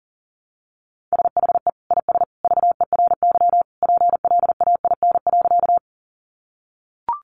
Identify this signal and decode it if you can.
Morse code, heard as a keyed tone, about 40 wpm, 700 Hz. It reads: SHE IS VERY PLAIN.
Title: Sentences from Top 400 Words 40wpm